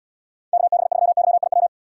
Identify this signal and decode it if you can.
Morse code, heard as a keyed tone, about 78 wpm, 700 Hz. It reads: KB3YIW